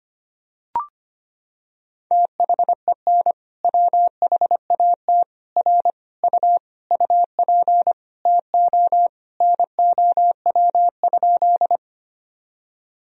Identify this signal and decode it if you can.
Morse code, heard as a keyed tone, about 25 wpm, 700 Hz. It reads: THEN WHAT R U UP TO NOW?
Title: Sentences from Top 100 Words and Abbreviations 25wpm